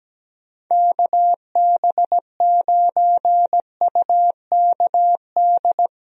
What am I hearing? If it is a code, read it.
Morse code, heard as a keyed tone, about 17 wpm, 700 Hz. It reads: KB9UKD
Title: USA Call Signs - 6 Characters - Repeated 3x - 17wpm